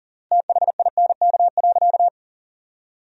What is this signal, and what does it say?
Morse code, heard as a keyed tone, about 40 wpm, 700 Hz. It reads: THINK.
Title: 3x Word-Spacing - Sentences from Top 500 Words - 40wpm